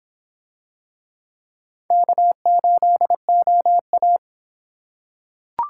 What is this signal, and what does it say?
Morse code, heard as a keyed tone, about 25 wpm, 700 Hz. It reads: K8OA